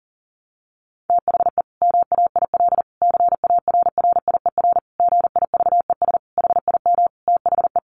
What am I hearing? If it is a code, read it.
Morse code, heard as a keyed tone, about 37 wpm, 700 Hz. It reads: THE MAIL CARRIER GIVES HIM THE
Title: Additional Easy Phrases 40wpm